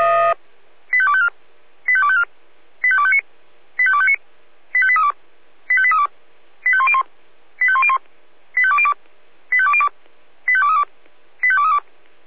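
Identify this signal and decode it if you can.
Five-tone selective call (ZVEI), heard as quick runs of five tones, tones of about 70 ms. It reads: 85254 85254 85259 85259 85832 85832 85101 85101 85202 85202 85232 85232
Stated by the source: ZVEI five-tone selective call received with an SDR.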